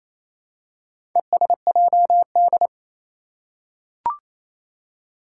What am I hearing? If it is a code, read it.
Morse code, heard as a keyed tone, about 27 wpm, 700 Hz. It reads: ESJD
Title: Four Challenging Letter-Number Combinations 28wpm